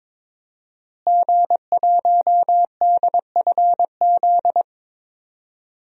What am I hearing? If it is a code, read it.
Morse code, heard as a keyed tone, about 22 wpm, 700 Hz. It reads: G1DFZ
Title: DXCC-100 Call Signs - 5 Characters - Repeated 3x - 22wpm